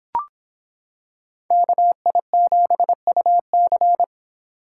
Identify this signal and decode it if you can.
Morse code, heard as a keyed tone, about 26 wpm, 700 Hz. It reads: KI7UC